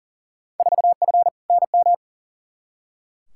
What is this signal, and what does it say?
Morse code, heard as a keyed tone, about 40 wpm, 700 Hz. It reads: 4F NM
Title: Field Day Exchanges Encore - 40wpm